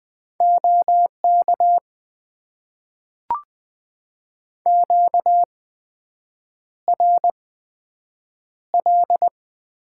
Morse code, heard as a keyed tone, about 20 wpm, spaced as for 7 wpm, 700 Hz. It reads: OK QRL
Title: Top 90 QSO Elements - 8x Character Spacing - 20wpm